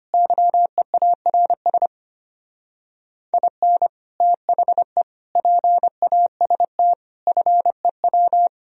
Morse code, heard as a keyed tone, about 31 wpm, 700 Hz. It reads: YEARS IN THE PAST FEW
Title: Speed-Racing - Sets of 5 Words 25wpm